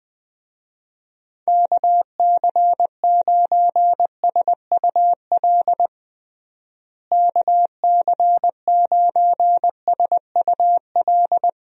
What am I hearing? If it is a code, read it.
Morse code, heard as a keyed tone, about 20 wpm, 700 Hz. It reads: KC9SUL KC9SUL
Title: USA Call Signs - 6 Characters - Repeated 2x - 20wpm